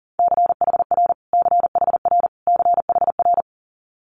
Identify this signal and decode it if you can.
Morse code, heard as a keyed tone, about 40 wpm, 690 Hz. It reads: CHR CHR CHR
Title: Multi-letter Phonemes Repeated 3x 40wpm